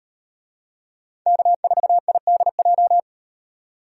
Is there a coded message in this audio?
Yes: Morse code from 1.3 to 3.0 s, heard as a keyed tone, about 38 wpm, 700 Hz. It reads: K4IDJ